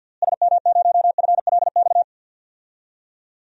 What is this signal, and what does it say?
Morse code, heard as a keyed tone, about 49 wpm, 700 Hz. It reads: SM0FLX
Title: DXCC-100 Call Signs - 6 Characters - Repeated 3x - 50wpm